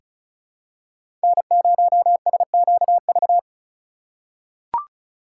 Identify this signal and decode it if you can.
Morse code, heard as a keyed tone, about 36 wpm, 700 Hz. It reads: N0SQV
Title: USA Call Signs - 4x Character Spacing - 35wpm